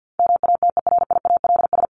Morse code, heard as a keyed tone, about 48 wpm, 700 Hz. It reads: MATERIALS
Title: ES:S Suffix - Top 100 Words with Base Words 50wpm